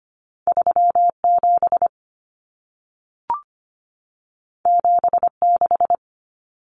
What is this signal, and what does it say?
Morse code, heard as a keyed tone, about 25 wpm, 700 Hz. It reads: 37 76
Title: Common Cut Numbers - 2-Digit 25wpm